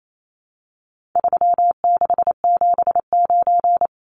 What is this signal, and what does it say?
Morse code, heard as a keyed tone, about 28 wpm, 700 Hz. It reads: 3679
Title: Numbers - 4-Digit 28wpm